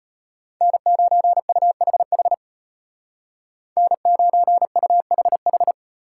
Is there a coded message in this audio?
Yes: Morse code from 0.6 to 5.7 s, heard as a keyed tone, about 32 wpm, 700 Hz. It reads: N9UHH N9UHH